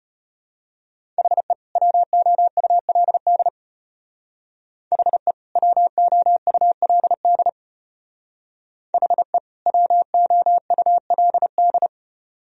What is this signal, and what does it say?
Morse code, heard as a keyed tone, about 32 wpm, 700 Hz. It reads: HE WOULD HE WOULD HE WOULD